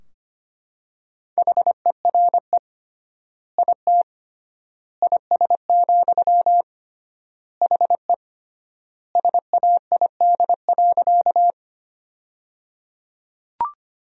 Morse code, heard as a keyed tone, about 25 wpm, 700 Hz. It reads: HERE IT IS, HE SAID.
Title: 3x Word-Spacing - Sentences from Top 200 Words - 25wpm